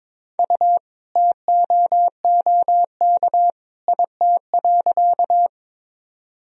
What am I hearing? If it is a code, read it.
Morse code, heard as a keyed tone, about 22 wpm, 700 Hz. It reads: U TOOK IT.
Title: Sentences from Top 300 Words and Abbreviations 22wpm